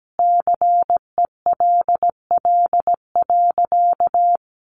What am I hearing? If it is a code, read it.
Morse code, heard as a keyed tone, about 17 wpm, 700 Hz. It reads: CELL.
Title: Sentences from Top 1100 Words 17wpm